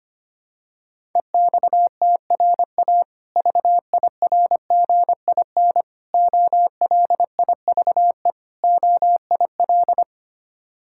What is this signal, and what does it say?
Morse code, heard as a keyed tone, about 25 wpm, 700 Hz. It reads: EXTRA VIRGIN OLIVE OIL
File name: practice / 25wpm / Difficult Words with Hints - 25wpm